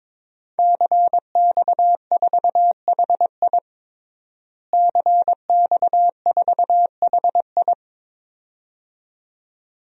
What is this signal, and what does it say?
Morse code, heard as a keyed tone, about 22 wpm, 700 Hz. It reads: CX4HI CX4HI
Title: DXCC-100 Call Signs - 5 Characters - Repeated 2x - 22wpm